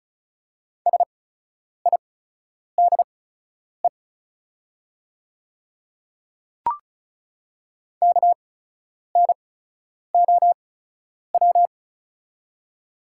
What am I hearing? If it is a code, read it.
Morse code, heard as a keyed tone, about 34 wpm, spaced as for 10 wpm, 700 Hz. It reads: SIDE KNOW
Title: 8x Character Spacing - Top 300 Words - 35wpm